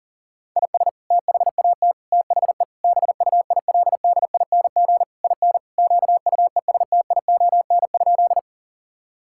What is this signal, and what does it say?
Morse code, heard as a keyed tone, about 37 wpm, 700 Hz. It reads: IS THAT THE BUILDING IN QUESTION?
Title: Sentences from Top 1000 Words - Review Part 5 - 40wpm